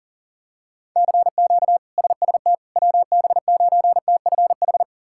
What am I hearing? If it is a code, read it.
Morse code, heard as a keyed tone, about 40 wpm, 700 Hz. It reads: CQ SST WB9TFH
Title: K1USN SST QSO Practice - 40wpm